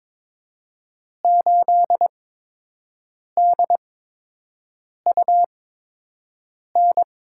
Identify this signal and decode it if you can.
Morse code, heard as a keyed tone, about 22 wpm, spaced as for 6 wpm, 700 Hz. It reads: 8DUN
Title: USA Call Signs - 5 Characters - 8x Character Spacing - 22wpm